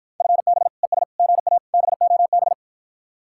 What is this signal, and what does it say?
Morse code, heard as a keyed tone, about 54 wpm, 700 Hz. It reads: FB ES GA BOB